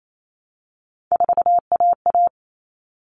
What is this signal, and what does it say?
Morse code, heard as a keyed tone, about 28 wpm, 700 Hz. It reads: 4AA
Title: All Cut Numbers - 3-Digit 28wpm